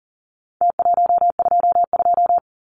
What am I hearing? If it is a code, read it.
Morse code, heard as a keyed tone, about 40 wpm, 700 Hz. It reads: T122
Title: Numbers - Contest Style - 4-Digit 40wpm